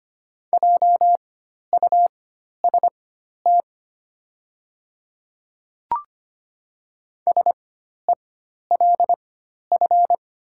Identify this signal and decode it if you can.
Morse code, heard as a keyed tone, about 25 wpm, spaced as for 12 wpm, 690 Hz. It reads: JUST SELF
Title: 4x Character Spacing - Top 200 Words - 25wpm